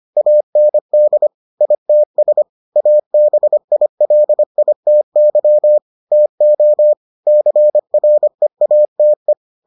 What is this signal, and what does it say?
Morse code, heard as a keyed tone, about 25 wpm, 600 Hz. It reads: AND ITS ABILITY TO CREATE